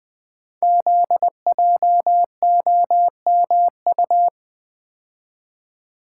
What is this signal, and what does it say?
Morse code, heard as a keyed tone, about 20 wpm, 700 Hz. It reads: ZJOMU
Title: Code Groups - 5 Letters - 20wpm